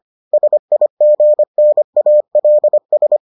Morse code, heard as a keyed tone, about 25 wpm, 600 Hz. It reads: SIGNALS